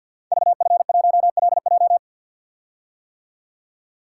Morse code, heard as a keyed tone, about 46 wpm, 700 Hz. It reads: VF1LJ
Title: Code Groups - 5 Letter:Numbers - 50wpm